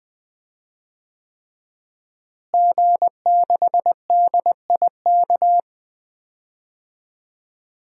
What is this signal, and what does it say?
Morse code, heard as a keyed tone, about 20 wpm, 700 Hz. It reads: G6DIK
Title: DXCC-100 Call Signs - 5 Characters - Repeated 3x - 20wpm